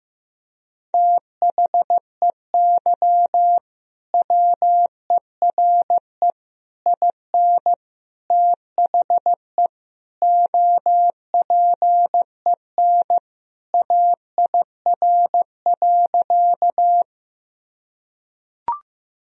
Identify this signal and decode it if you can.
Morse code, heard as a keyed tone, about 15 wpm, 700 Hz. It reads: THEY WERE IN THE OPEN AIR.
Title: Sentences from Top 1000 Words - Review Part 1 - 15wpm